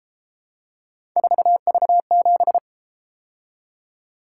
Morse code, heard as a keyed tone, about 31 wpm, 700 Hz. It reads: 4V7